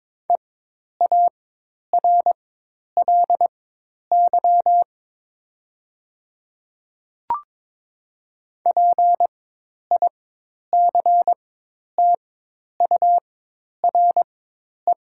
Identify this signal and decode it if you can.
Morse code, heard as a keyed tone, about 21 wpm, spaced as for 10 wpm, 700 Hz. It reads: EARLY PICTURE
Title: 4x Character Spacing - Top 500 Words - 22wpm